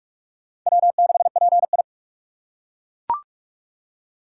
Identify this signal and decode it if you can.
Morse code, heard as a keyed tone, about 41 wpm, 700 Hz. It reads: W6PI
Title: USA Call Signs - 4 Characters - 4x Character Spacing - 45wpm